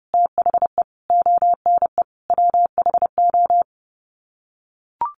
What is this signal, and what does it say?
Morse code, heard as a keyed tone, about 30 wpm, 700 Hz. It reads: THE ONE WHO
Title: Sets of 5 Words Encore2 30wpm